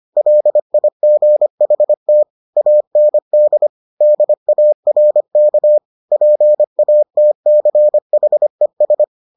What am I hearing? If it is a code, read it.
Morse code, heard as a keyed tone, about 25 wpm, 600 Hz. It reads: LIGHT AND DARK PATCHES